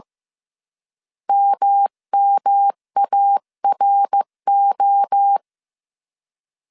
Morse code, heard as a keyed tone, about 15 wpm, 800 Hz. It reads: MMARO
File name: CW / DET / 1 3 1500C S A 14wpm 2447